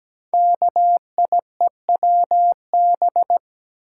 Morse code, heard as a keyed tone, about 17 wpm, 700 Hz. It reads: KIEWB
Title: Code Groups - 5 Letters - 17wpm